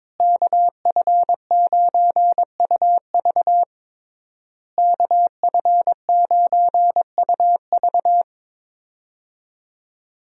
Morse code, heard as a keyed tone, about 22 wpm, 700 Hz. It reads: KF9UV KF9UV
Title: USA Call Signs - 5 Characters - Repeated 3x - 22wpm